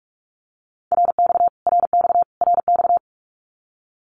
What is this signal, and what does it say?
Morse code, heard as a keyed tone, about 44 wpm, 700 Hz. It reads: RX RX RX
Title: Abbreviations Repeated 3x - 45wpm